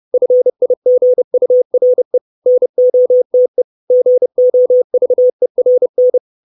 Morse code, heard as a keyed tone, about 30 wpm, 500 Hz. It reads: FIGURE NOTE GOVERN
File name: 632 Words 30WPM